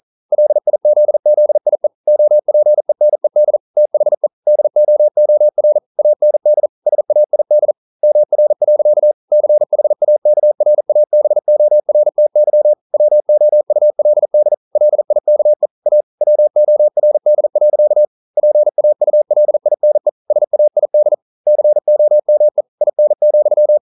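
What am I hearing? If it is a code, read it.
Morse code, heard as a keyed tone, about 42 wpm, 600 Hz. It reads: LIZZIE OPENED THE DOOR AND SAID MR. CHAKRABORTY WOULD LIKE A WORD. PAULINE SAID COME IN,
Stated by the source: Morse code practice at 41 wpm, from CW med Gnister.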